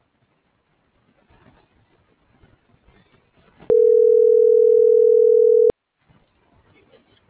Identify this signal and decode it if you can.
Ringback tone (North America), 440 + 480 Hz.